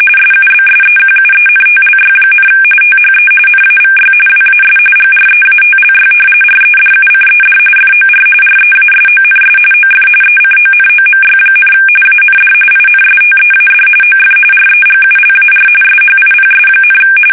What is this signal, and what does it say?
RTTY, about 45.45 baud. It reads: SUBSEQUENTLY, THE NAVY WAS GREATLY REDUCED FOR REASONS OF ECONOMY, AND INSTEAD OF REGULAR SHIPS,
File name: fsk850 / usnavy1